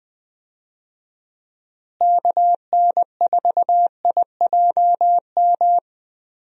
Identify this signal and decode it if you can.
Morse code, heard as a keyed tone, about 20 wpm, 700 Hz. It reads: KN4IJM
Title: USA Call Signs - 6 Characters - Repeated 2x - 20wpm